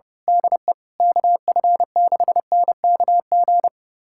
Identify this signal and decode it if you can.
Morse code, heard as a keyed tone, about 30 wpm, 700 Hz. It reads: DE KF6NKG
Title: USA Callsign DE Callsign - 6 Characters - 30wpm